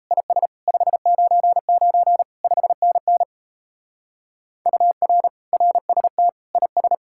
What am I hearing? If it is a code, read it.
Morse code, heard as a keyed tone, about 36 wpm, 700 Hz. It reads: IS 599 5NN UR RST IS